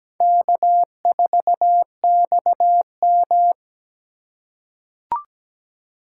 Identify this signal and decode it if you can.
Morse code, heard as a keyed tone, about 17 wpm, 700 Hz. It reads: K4XM
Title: USA Call Signs - 4 Characters - 4x Character Spacing - 17wpm